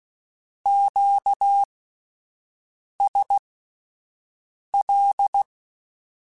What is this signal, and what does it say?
Morse code, heard as a keyed tone, about 16 wpm, spaced as for 5 wpm, 800 Hz. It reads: QSL
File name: (spaced) / Q-Calls Spaced-mp3s / Q Calls sp Once @10